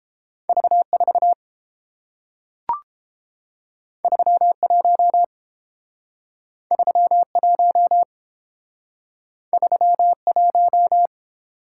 Morse code, heard as a keyed tone, about 30 wpm, 700 Hz. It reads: V4 31 31 31